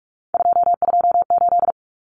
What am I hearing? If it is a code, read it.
Morse code, heard as a keyed tone, about 44 wpm, 700 Hz. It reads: 228
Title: Numbers - 3-Digit 45wpm